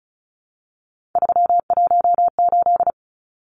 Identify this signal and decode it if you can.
Morse code, heard as a keyed tone, about 34 wpm, 700 Hz. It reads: 318
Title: Numbers - 3-Digit 35wpm